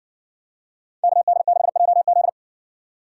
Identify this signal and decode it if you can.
Morse code, heard as a keyed tone, about 60 wpm, 700 Hz. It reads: KD6JZ